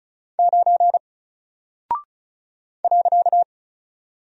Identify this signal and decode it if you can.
Morse code, heard as a keyed tone, about 36 wpm, 700 Hz. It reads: 9 .
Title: Single Character (Letter:Number:Punctuation:Prosign) 35wpm